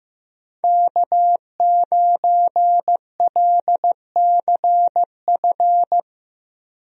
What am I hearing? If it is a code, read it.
Morse code, heard as a keyed tone, about 15 wpm, 700 Hz. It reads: K9LCF
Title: USA Call Signs - 5 Characters - Repeated 2x - 15wpm